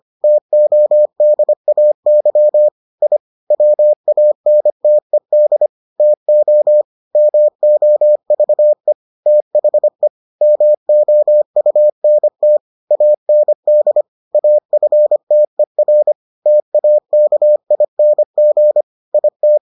Morse code, heard as a keyed tone, about 25 wpm, 600 Hz. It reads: TODAY I WANTED TO MOVE THE MOUNT AND AFTER TAKING IT